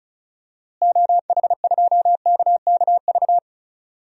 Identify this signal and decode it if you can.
Morse code, heard as a keyed tone, about 36 wpm, 700 Hz. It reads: OH2KKV
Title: DXCC-100 Call Signs - 6 Characters - Repeated 3x - 35wpm